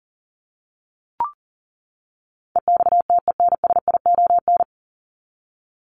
Morse code, heard as a keyed tone, about 40 wpm, 700 Hz. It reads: EXTENSION
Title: Top 1501-1750 Words 40wpm